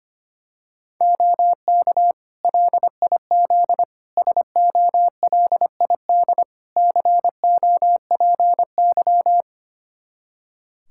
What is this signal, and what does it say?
Morse code, heard as a keyed tone, about 25 wpm, 700 Hz. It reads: OK LIZ SOLID COPY
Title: K1USN SST QSO Practice 25wpm